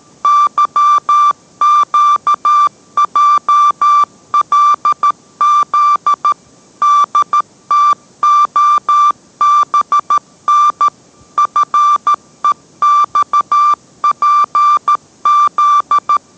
Morse code, heard as a keyed tone, about 13 wpm, 1180 Hz. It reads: YQJLZ DTOBN FEXPZ